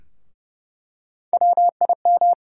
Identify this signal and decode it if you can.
Morse code, heard as a keyed tone, about 29 wpm, 700 Hz. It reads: WIM
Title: USA Call Sign Postfixes - 3 Letters 30wpm